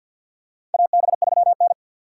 Morse code, heard as a keyed tone, about 47 wpm, 700 Hz. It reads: AB3N
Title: USA Call Signs - 4 Characters - 6x Character Spacing - 50wpm